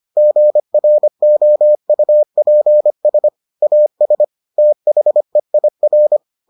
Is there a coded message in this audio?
Yes: Morse code from 0.2 to 6.2 s, heard as a keyed tone, about 25 wpm, 600 Hz. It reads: GROUPS AS THEIR